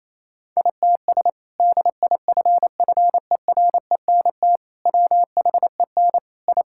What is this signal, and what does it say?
Morse code, heard as a keyed tone, about 27 wpm, 700 Hz. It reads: ITS DIFFERENT WHEN I
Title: Sentences from Top 1000 Words - Review Part 5 - 28wpm